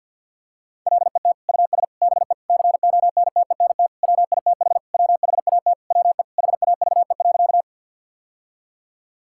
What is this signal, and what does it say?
Morse code, heard as a keyed tone, about 51 wpm, 700 Hz. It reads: LET US BE CONTENT WITH WHAT WE HAVE.
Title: Sentences from Top 1000 Words - Review Part 6 - 50wpm